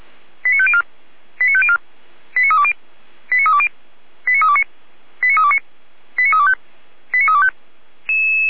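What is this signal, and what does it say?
Five-tone selective call (ZVEI), heard as quick runs of five tones, tones of about 70 ms. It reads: 89504 89504 89320 89320 89329 89329 89325 89325
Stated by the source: Received on SDR; station feuerwehr-orlamuende.org (fire brigade Orlamünde).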